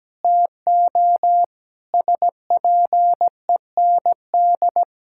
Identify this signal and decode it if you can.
Morse code, heard as a keyed tone, about 17 wpm, 700 Hz. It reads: TO SPEND
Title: Sentences from Top 1000 Words - Review Part 5 - 17wpm